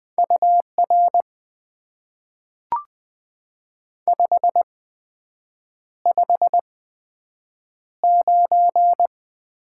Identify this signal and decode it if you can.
Morse code, heard as a keyed tone, about 20 wpm, spaced as for 7 wpm, 700 Hz. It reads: UR 559
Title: Top 90 QSO Elements - 8x Character Spacing - 20wpm